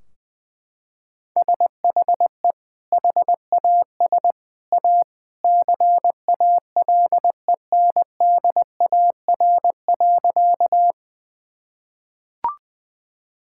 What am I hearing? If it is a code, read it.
Morse code, heard as a keyed tone, about 20 wpm, 700 Hz. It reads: SHE HAS A CALENDAR.
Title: Sentences from Top 1600 Words 20wpm